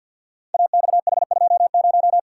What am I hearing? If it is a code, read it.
Morse code, heard as a keyed tone, about 53 wpm, 700 Hz. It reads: AXH20
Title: Code Groups - 5 Letter:Numbers - 50wpm